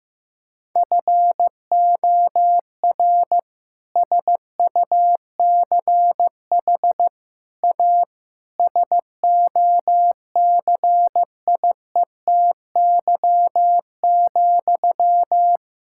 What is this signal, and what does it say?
Morse code, heard as a keyed tone, about 15 wpm, 700 Hz. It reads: FOR SUCH A SOCIETY,
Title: Sentences from Top 1000 Words - Review Part 6 - 15wpm